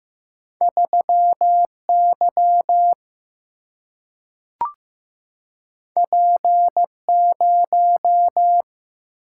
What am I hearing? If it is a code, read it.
Morse code, heard as a keyed tone, about 15 wpm, 700 Hz. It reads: 3Y P0
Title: Single-Letter Single-Number 15wpm